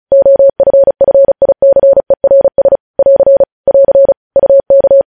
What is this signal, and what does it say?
Morse code, heard as a keyed tone, about 36 wpm, 550 Hz. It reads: OFFICERS <AR> <AR> UK
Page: News Headlines in Morse Code at 35 WPM